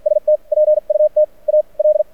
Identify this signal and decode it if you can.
Morse code, heard as a keyed tone, about 52 wpm, spaced as for 40 wpm, 590 Hz. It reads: ST PUT AL
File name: archived / CRSnet